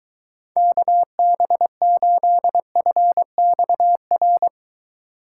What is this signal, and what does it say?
Morse code, heard as a keyed tone, about 22 wpm, 700 Hz. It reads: KB8FXR